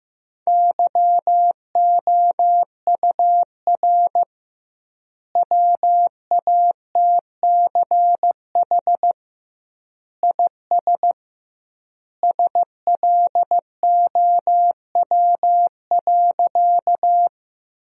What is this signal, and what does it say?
Morse code, heard as a keyed tone, about 15 wpm, 700 Hz. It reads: YOUR WATCH IS SLOW.
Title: 2x Word-Spacing - Sentences from Top 400 Words - 15wpm